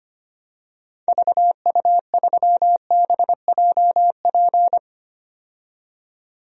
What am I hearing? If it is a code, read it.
Morse code, heard as a keyed tone, about 25 wpm, 700 Hz. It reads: VU3BJP
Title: DXCC-100 Call Signs - 6 Characters - 25wpm